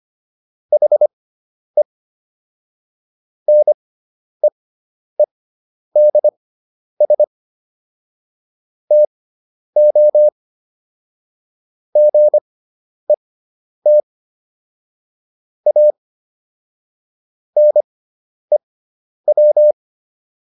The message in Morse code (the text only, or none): HE NEEDS TO GET A NEW